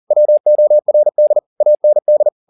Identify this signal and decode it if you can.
Morse code, heard as a keyed tone, about 40 wpm, 600 Hz. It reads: WORD AND
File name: the old man and the sea 2 40wpm